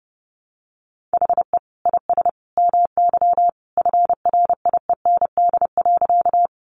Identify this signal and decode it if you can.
Morse code, heard as a keyed tone, about 28 wpm, 700 Hz. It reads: HE IS MY FRIEND.
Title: Sentences from Top 300 Words 30wpm